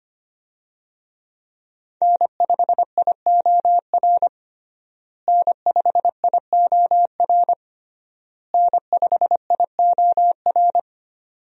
Morse code, heard as a keyed tone, about 25 wpm, 700 Hz. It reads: N5IOR N5IOR N5IOR
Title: USA Call Signs - 5 Characters - Repeated 3x - 25wpm